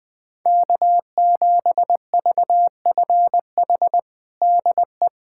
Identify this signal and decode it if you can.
Morse code, heard as a keyed tone, about 20 wpm, 700 Hz. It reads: K7VFH DE
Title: USA Callsign DE Callsign - 5 Characters - 20wpm